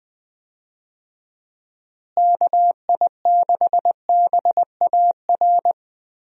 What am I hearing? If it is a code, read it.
Morse code, heard as a keyed tone, about 21 wpm, 700 Hz. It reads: KI6BAR
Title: USA Call Signs - 6 Characters - Repeated 2x - 20wpm